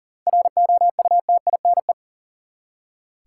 Morse code, heard as a keyed tone, about 40 wpm, 700 Hz. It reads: ROUTINE
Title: Seven Letter Words - 40wpm